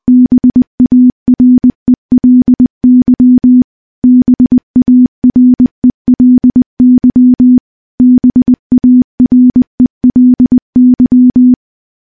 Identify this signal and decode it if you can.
Morse code, heard as a keyed tone, about 20 wpm, 250 Hz. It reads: BARELY BARELY BARELY
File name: morse / morse-training-20WPM